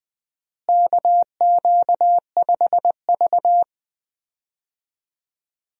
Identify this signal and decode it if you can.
Morse code, heard as a keyed tone, about 20 wpm, 700 Hz. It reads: KQ5V
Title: USA Call Signs - 4 Characters - Repeated 2x - 20wpm